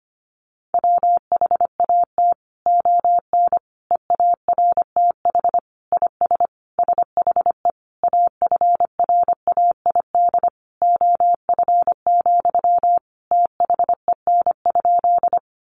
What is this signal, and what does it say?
Morse code, heard as a keyed tone, about 25 wpm, 700 Hz. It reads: WHAT ON EARTH IS SHE AFRAID OF, THEN?
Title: Sentences from Top 1500 Words 25wpm